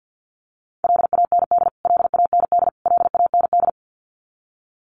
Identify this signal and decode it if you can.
Morse code, heard as a keyed tone, about 48 wpm, 700 Hz. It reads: LAND LAND LAND
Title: Top 200 Words Repeated 50wpm